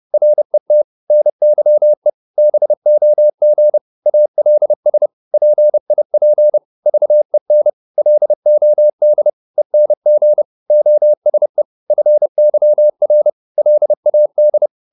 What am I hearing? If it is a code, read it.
Morse code, heard as a keyed tone, about 30 wpm, 600 Hz. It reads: RET NYE BOG ALS PIP VEN LOD ENG OSE FYR LAD
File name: Korte ord DK 30wpm